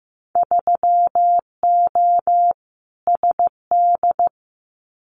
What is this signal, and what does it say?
Morse code, heard as a keyed tone, about 15 wpm, 700 Hz. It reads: 3O SD
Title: Winter Field Day Exchanges 15wpm